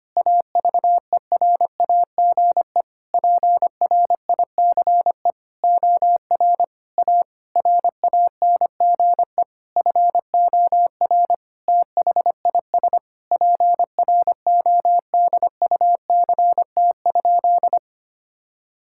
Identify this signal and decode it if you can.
Morse code, heard as a keyed tone, about 25 wpm, 700 Hz. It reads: AVERAGE PRICE OR A RANGE FOR THIS PRODUCT?
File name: practice / 25wpm / Sentences with 9-Word Phrases - 25wpm